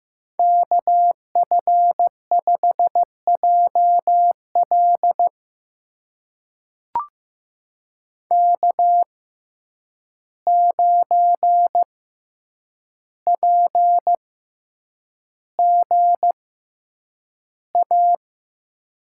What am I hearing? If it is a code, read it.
Morse code, heard as a keyed tone, about 15 wpm, spaced as for 8 wpm, 700 Hz. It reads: KF5JL K9PGA
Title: USA Call Signs - 5 Characters - 6x Character Spacing - 15wpm